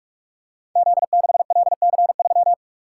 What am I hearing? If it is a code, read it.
Morse code, heard as a keyed tone, about 44 wpm, 700 Hz. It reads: Z6LC3
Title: Code Groups - 5 Letter:Numbers - 45wpm